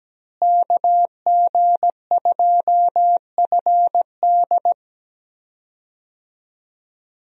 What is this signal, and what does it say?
Morse code, heard as a keyed tone, about 17 wpm, 700 Hz. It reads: KG2FD
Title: USA Call Signs - 5 Characters - Repeated 2x - 17wpm